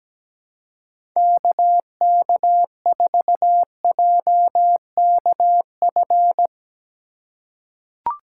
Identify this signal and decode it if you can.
Morse code, heard as a keyed tone, about 17 wpm, 700 Hz. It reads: KK4JKF
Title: USA Call Signs - 6 Characters - 2x Character Spacing - 17wpm